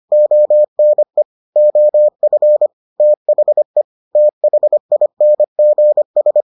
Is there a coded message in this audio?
Yes: Morse code from 0.1 to 6.4 s, heard as a keyed tone, about 25 wpm, 600 Hz. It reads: ONE OF THE THINGS